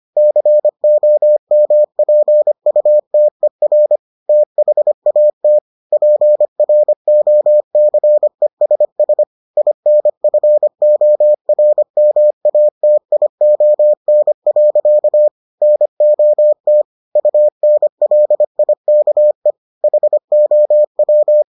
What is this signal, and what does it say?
Morse code, heard as a keyed tone, about 25 wpm, 600 Hz. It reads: COMPUTER THAT PROCESS INFORMATION. NOT UNLIKE HOW